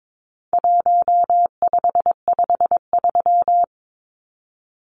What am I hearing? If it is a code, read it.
Morse code, heard as a keyed tone, about 22 wpm, 700 Hz. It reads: 1553
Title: Numbers - 4-Digit 22wpm